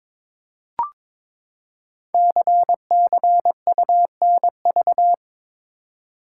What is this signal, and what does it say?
Morse code, heard as a keyed tone, about 22 wpm, 700 Hz. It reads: CCUNV
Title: Code Groups - 5 Letters - 22wpm